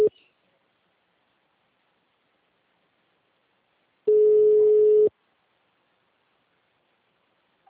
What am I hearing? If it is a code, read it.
Ringback tone (Europe, 425 Hz).
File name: out- -2011- - - .2880